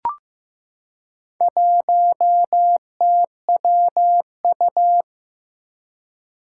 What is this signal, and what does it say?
Morse code, heard as a keyed tone, about 15 wpm, 700 Hz. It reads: 1TWU
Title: Four Challenging Letter-Number Combinations 15wpm